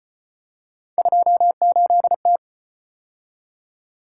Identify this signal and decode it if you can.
Morse code, heard as a keyed tone, about 35 wpm, 700 Hz. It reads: 28T